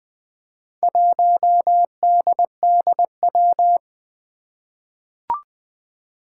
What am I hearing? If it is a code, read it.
Morse code, heard as a keyed tone, about 20 wpm, 700 Hz. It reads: 1DDW
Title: Four Challenging Letter-Number Combinations 20wpm